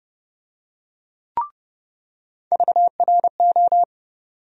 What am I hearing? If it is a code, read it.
Morse code, heard as a keyed tone, about 30 wpm, 700 Hz. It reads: VRO
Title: USA Call Sign Postfixes - 3 Letters 30wpm